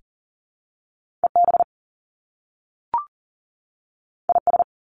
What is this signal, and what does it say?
Morse code, heard as a keyed tone, about 40 wpm, 700 Hz. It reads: EB IS
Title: Top Two Letter Combinations in Words 40wpm